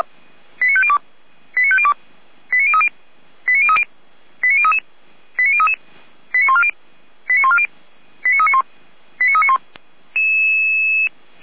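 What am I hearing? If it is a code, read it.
Five-tone selective call (ZVEI), heard as quick runs of five tones, tones of about 70 ms. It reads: 89502 89502 89030 89030 89033 89033 89140 89140 89391 89391
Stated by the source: SDR-received ZVEI five-tone selective call, address 89042.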